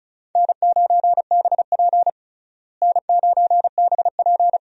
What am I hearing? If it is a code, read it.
Morse code, heard as a keyed tone, about 34 wpm, 700 Hz. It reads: N9BP N9BP
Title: USA Call Signs - 4 Characters - Repeated 2x - 35wpm